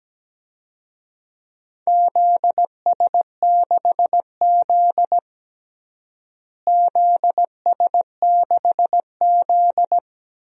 Morse code, heard as a keyed tone, about 17 wpm, 700 Hz. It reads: ZS6Z ZS6Z
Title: DXCC-100 Call Signs - 4 Characters - Repeated 2x - 17wpm